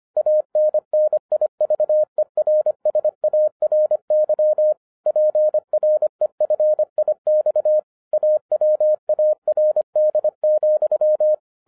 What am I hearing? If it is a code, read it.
Morse code, heard as a keyed tone, about 25 wpm, 610 Hz. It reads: ANNIVERSARY PREFIX AWARD,
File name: 2012-01-mp3 / n14-FUN WITH NUMBERS-25wpm